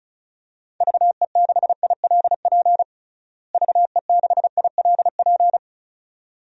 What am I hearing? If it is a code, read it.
Morse code, heard as a keyed tone, about 36 wpm, 700 Hz. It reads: VE6ILP VE6ILP
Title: DXCC-100 Call Signs - 6 Characters - Repeated 2x - 35wpm